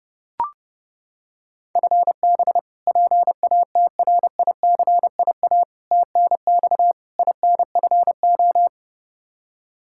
Morse code, heard as a keyed tone, about 30 wpm, 700 Hz. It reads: FB PATRICIA TNX INFO
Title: Practice QSO - Exchange 3 - More Info - 30wpm